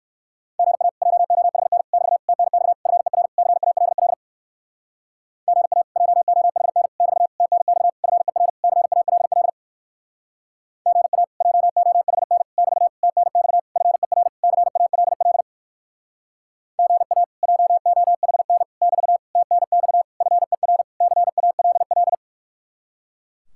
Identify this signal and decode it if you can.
Morse code, heard as a keyed tone, about 59 wpm, 700 Hz. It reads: GA JOHN <BT> TNX FER CALL GA JOHN <BT> TNX FER CALL GA JOHN <BT> TNX FER CALL GA JOHN <BT> TNX FER CALL